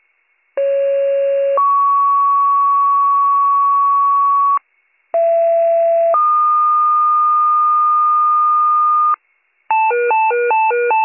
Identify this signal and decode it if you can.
Two-tone page, heard as a one-second tone, then a three-second tone, about 550 then 1120 Hz, about 670 then 1220 Hz. It is heard from 0.6 to 9.1 s.